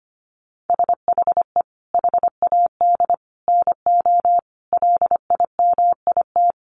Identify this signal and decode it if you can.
Morse code, heard as a keyed tone, about 25 wpm, 700 Hz. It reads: SHE HAD NO LIMIT
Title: Sentences from Top 1500 Words 25wpm